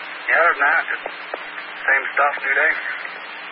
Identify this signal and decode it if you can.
Emergency Alert System (SAME) tones, heard as harsh buzzing bursts, with speech. It is heard from 0.5 to 3.1 s.